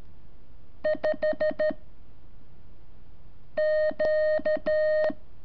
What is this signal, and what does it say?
Morse code, heard as a keyed tone, about 13 wpm, 620 Hz. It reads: 5 Q
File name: curso / ejercicio 26